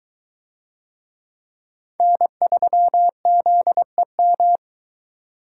Morse code, heard as a keyed tone, about 21 wpm, 700 Hz. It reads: N3ZEM